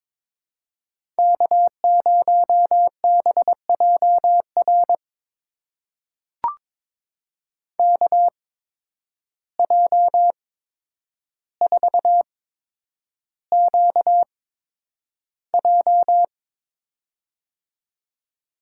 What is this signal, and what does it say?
Morse code, heard as a keyed tone, about 22 wpm, spaced as for 9 wpm, 700 Hz. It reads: K0BJR KJ4QJ
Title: USA Call Signs - 5 Characters - 8x Character Spacing - 22wpm